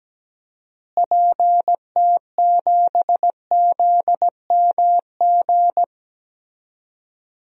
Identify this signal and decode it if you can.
Morse code, heard as a keyed tone, about 17 wpm, 700 Hz. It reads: PT7ZMG
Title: DXCC-100 Call Signs - 6 Characters - Repeated 2x - 17wpm